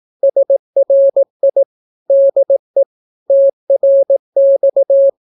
Morse code, heard as a keyed tone, about 18 wpm, 550 Hz. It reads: SRI DE TRX
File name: naukapodstaw / skroty / skroty